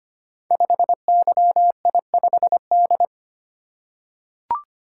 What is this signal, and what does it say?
Morse code, heard as a keyed tone, about 25 wpm, 700 Hz. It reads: 5YI5D
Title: Code Groups - 5 Letter:Numbers - 25wpm